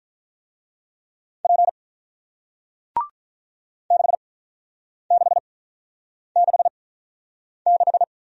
Morse code, heard as a keyed tone, about 48 wpm, spaced as for 9 wpm, 700 Hz. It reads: P 6666